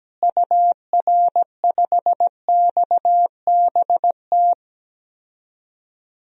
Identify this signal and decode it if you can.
Morse code, heard as a keyed tone, about 17 wpm, 700 Hz. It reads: UR5XBT
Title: DXCC-100 Call Signs - 6 Characters - Repeated 3x - 17wpm